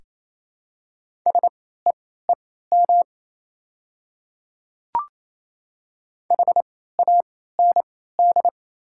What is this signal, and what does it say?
Morse code, heard as a keyed tone, about 28 wpm, spaced as for 16 wpm, 700 Hz. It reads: SEEM HAND
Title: 3x Character Spacing - Top 300 Words - 28wpm